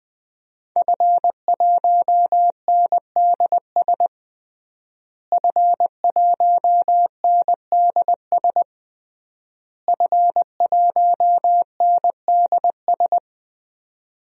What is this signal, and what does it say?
Morse code, heard as a keyed tone, about 20 wpm, 700 Hz. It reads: F1NDS F1NDS F1NDS
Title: DXCC-100 Call Signs - 5 Characters - Repeated 3x - 20wpm